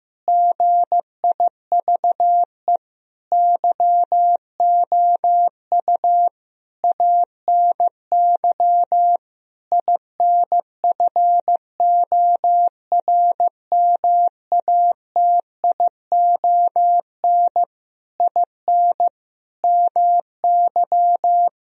Morse code, heard as a keyed tone, about 15 wpm, 700 Hz. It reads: GIVE YOU ANY INFORMATION IN MY
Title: Sentences from Top 1000 Words - Review Part 3 - 15wpm